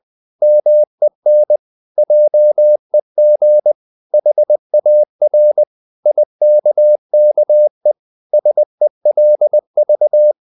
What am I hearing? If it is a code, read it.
Morse code, heard as a keyed tone, about 20 wpm, 600 Hz. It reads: MEN JEG HAR IKKE SELV
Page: Ferie 20 wpm